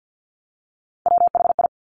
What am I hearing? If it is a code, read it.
Morse code, heard as a keyed tone, about 50 wpm, 700 Hz. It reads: WHI
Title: Top Three Letter Combinations in Words 50wpm